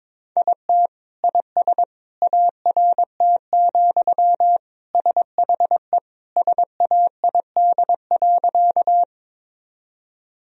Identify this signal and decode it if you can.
Morse code, heard as a keyed tone, about 23 wpm, 690 Hz. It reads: IT IS ART, SHE SAID.
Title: Sentences from Top 1000 Words - Review Part 3 - 22wpm